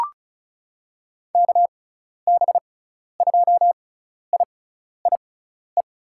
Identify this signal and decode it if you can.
Morse code, heard as a keyed tone, about 36 wpm, spaced as for 12 wpm, 700 Hz. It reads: KB2IIE
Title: USA Call Signs - 6 Characters - 6x Character Spacing - 35wpm